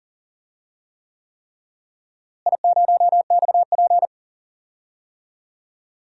Morse code, heard as a keyed tone, about 37 wpm, 700 Hz. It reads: I0XP
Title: DXCC-100 Call Signs - 4 Characters - Repeated 3x - 40wpm